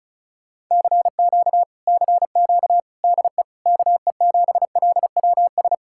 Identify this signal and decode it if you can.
Morse code, heard as a keyed tone, about 46 wpm, spaced as for 41 wpm, 700 Hz. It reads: CQ CQ DE KE7LWS
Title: Speed-Racing Common QSO Phrases - 35wpm